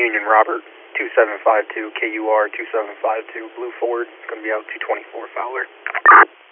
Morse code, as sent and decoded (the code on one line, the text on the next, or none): none